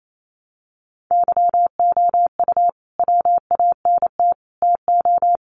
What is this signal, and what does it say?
Morse code, heard as a keyed tone, about 28 wpm, 700 Hz. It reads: YOU WANT TO
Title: Sets of 5 Words Encore 28wpm